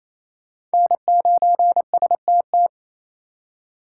Morse code, heard as a keyed tone, about 35 wpm, 700 Hz. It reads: N9STT
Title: Speed Racing - USA 1x3 Call Signs 28wpm